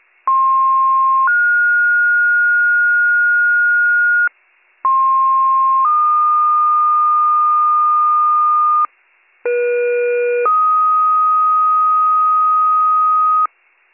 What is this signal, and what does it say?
Two-tone page, heard as a one-second tone, then a three-second tone, about 1060 then 1430 Hz, about 1060 then 1220 Hz, about 480 then 1250 Hz, with noise. It is heard from 0.3 to 13.5 s.